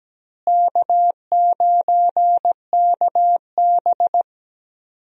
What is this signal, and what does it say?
Morse code, heard as a keyed tone, about 17 wpm, 700 Hz. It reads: K9KB
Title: USA Call Signs - 4 Characters - Repeated 3x - 17wpm